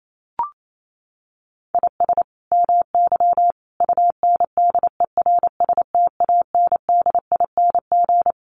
Morse code, heard as a keyed tone, about 28 wpm, 700 Hz. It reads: IS MY UNDERSTANDING
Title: Sentences from Top 1400 Words 28wpm